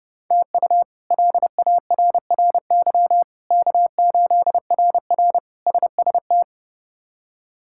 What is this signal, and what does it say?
Morse code, heard as a keyed tone, about 29 wpm, 700 Hz. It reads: TU LARRY K8RR SST
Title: K1USN SST QSO Practice - 30wpm